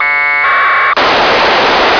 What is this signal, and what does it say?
Packet radio (AX.25, as in APRS).